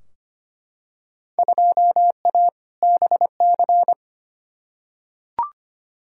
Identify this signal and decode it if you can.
Morse code, heard as a keyed tone, about 25 wpm, 700 Hz. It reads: 2A BC
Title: Field Day Exchanges - 25wpm